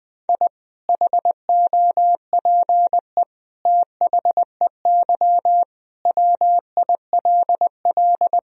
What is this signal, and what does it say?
Morse code, heard as a keyed tone, about 20 wpm, 700 Hz. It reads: I HOPE THEY WILL
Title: Sentences from Top 1000 Words - Review Part 2 - 20wpm